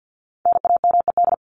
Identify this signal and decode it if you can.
Morse code, heard as a keyed tone, about 50 wpm, 700 Hz. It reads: NAMED
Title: ED Suffix - Top 100 Words with Base Words 50wpm